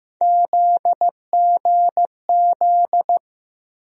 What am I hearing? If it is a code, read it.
Morse code, heard as a keyed tone, about 15 wpm, 700 Hz. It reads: ZGZ
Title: USA Call Sign Postfixes - 3 Letters 15wpm